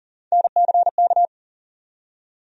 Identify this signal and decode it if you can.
Morse code, heard as a keyed tone, about 39 wpm, 700 Hz. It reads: NCK
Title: USA Call Sign Postfixes - 3 Letters 40wpm